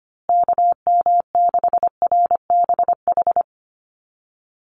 Morse code, heard as a keyed tone, about 25 wpm, 700 Hz. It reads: KM6RBH
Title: USA Call Signs Repeated 3x 25wpm